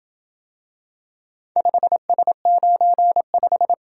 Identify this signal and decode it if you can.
Morse code, heard as a keyed tone, about 27 wpm, 700 Hz. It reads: 5S95